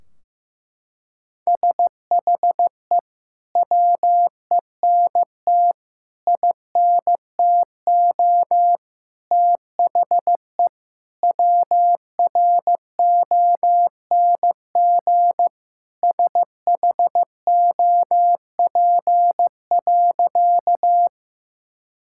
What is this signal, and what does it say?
Morse code, heard as a keyed tone, about 15 wpm, 700 Hz. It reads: SHE WENT INTO THE WRONG SHOP.
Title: Sentences from Top 1600 Words 15wpm